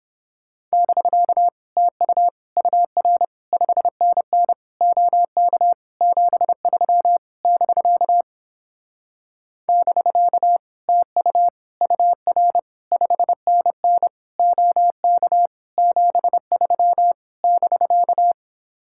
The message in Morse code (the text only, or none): <BK> TU UR 5NN OK 73 <BK> <BK> TU UR 5NN OK 73 <BK>